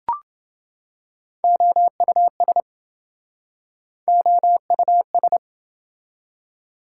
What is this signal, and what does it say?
Morse code, heard as a keyed tone, about 29 wpm, 700 Hz. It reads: OUS OUS